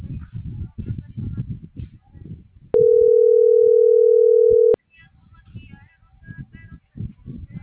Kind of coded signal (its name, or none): telephone call progress tone